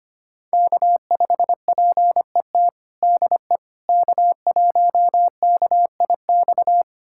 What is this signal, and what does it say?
Morse code, heard as a keyed tone, about 25 wpm, 700 Hz. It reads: K5PET DE K1KIX
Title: USA Callsign DE Callsign - 5 Characters - 25wpm